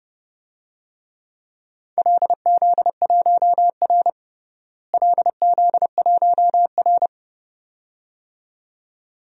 Morse code, heard as a keyed tone, about 32 wpm, 700 Hz. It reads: LZ1R LZ1R
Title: DXCC-100 Call Signs - 4 Characters - Repeated 2x - 30wpm